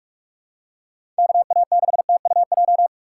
Morse code, heard as a keyed tone, about 48 wpm, 700 Hz. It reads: KA6TUJ
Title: USA Call Signs - 6 Characters - 3x Character Spacing - 45wpm